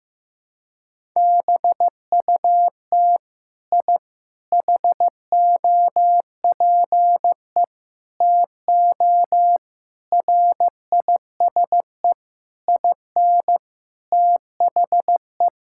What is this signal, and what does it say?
Morse code, heard as a keyed tone, about 16 wpm, 700 Hz. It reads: BUT I HOPE TO RISE IN THE
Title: Sentences from Top 1000 Words - Review Part 6 - 15wpm